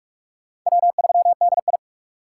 Morse code, heard as a keyed tone, about 46 wpm, 690 Hz. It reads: W3DI
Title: USA Call Signs - 4 Characters - 4x Character Spacing - 45wpm